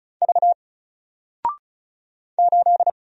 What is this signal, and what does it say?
Morse code, heard as a keyed tone, about 35 wpm, 700 Hz. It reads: V 8
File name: Single-Letter-Number-35wpm